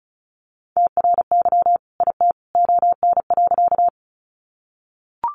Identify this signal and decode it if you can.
Morse code, heard as a keyed tone, about 34 wpm, 700 Hz. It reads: TRY IT ON.
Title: Sentences from Top 200 Words 35wpm